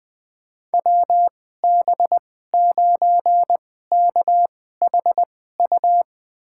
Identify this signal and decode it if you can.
Morse code, heard as a keyed tone, about 20 wpm, spaced as for 15 wpm, 700 Hz. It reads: WB9KHU
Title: USA Call Signs - 6 Characters - 2x Character Spacing - 20wpm